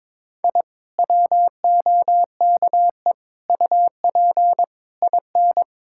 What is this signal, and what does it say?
Morse code, heard as a keyed tone, about 22 wpm, 700 Hz. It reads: I WOKE UP IN
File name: Sentences with 7-Word Phrases - 22wpm